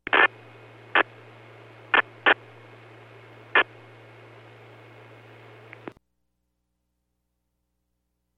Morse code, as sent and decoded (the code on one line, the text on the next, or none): none